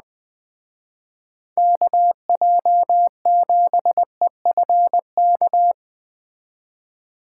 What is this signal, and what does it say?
Morse code, heard as a keyed tone, about 20 wpm, 700 Hz. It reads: KJ7EFK